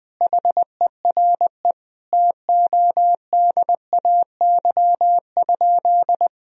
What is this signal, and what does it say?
Morse code, heard as a keyed tone, about 20 wpm, 700 Hz. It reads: HERE TODAY?
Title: Sentences from Top 1000 Words - Review Part 3 - 20wpm